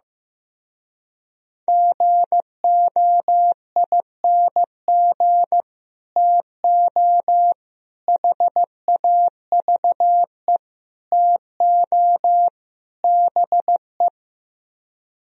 Morse code, heard as a keyed tone, about 15 wpm, 700 Hz. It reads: GOING TO HAVE TO BE
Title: Speed-Racing - Sets of 5 Words 15wpm